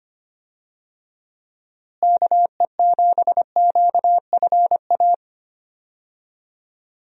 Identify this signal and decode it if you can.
Morse code, heard as a keyed tone, about 25 wpm, 700 Hz. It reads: KE7QFA
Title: USA Call Signs - 6 Characters - Repeated 2x - 25wpm